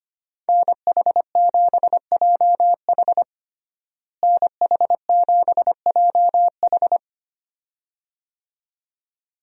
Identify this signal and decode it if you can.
Morse code, heard as a keyed tone, about 25 wpm, 700 Hz. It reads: NH7JH NH7JH
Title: USA Call Signs - 5 Characters - Repeated 2x - 25wpm